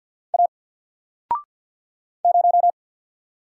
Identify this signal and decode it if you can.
Morse code, heard as a keyed tone, about 50 wpm, 700 Hz. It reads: A 0
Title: Single Letter-Number Alt 50wpm